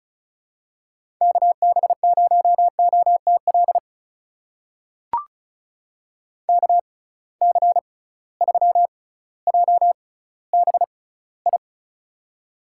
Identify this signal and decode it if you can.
Morse code, heard as a keyed tone, about 36 wpm, spaced as for 18 wpm, 700 Hz. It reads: KB0OTL KC3JBI
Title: USA Call Signs - 6 Characters - 6x Character Spacing - 35wpm